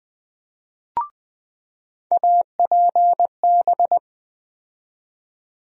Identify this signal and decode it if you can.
Morse code, heard as a keyed tone, about 20 wpm, 700 Hz. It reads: APB
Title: USA Call Sign Postfixes - 3 Letters 20wpm